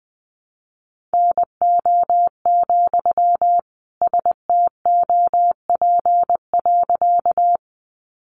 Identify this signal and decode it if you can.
Morse code, heard as a keyed tone, about 20 wpm, 700 Hz. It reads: NO, STOP.
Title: Sentences from Top 300 Words 20wpm